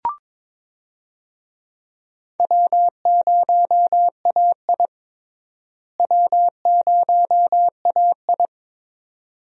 Morse code, heard as a keyed tone, about 22 wpm, 700 Hz. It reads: W0AI W0AI
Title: USA Call Signs - 4 Characters - Repeated 2x - 22wpm